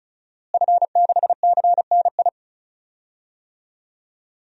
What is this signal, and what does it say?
Morse code, heard as a keyed tone, about 36 wpm, 700 Hz. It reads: F6CNI
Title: DXCC-100 Call Signs - 5 Characters - Repeated 2x - 35wpm